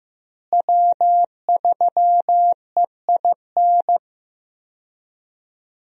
Morse code, heard as a keyed tone, about 15 wpm, 700 Hz. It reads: W3EIN
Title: USA Call Signs - 5 Characters - Repeated 3x - 15wpm